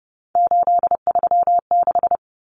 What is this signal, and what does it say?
Morse code, heard as a keyed tone, about 30 wpm, 700 Hz. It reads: 836
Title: Numbers - 3-Digit 30wpm